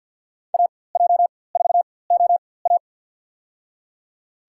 Morse code, heard as a keyed tone, about 53 wpm, spaced as for 24 wpm, 700 Hz. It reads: AJ4OA
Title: USA Call Signs - 5 Characters - 4x Character Spacing - 50wpm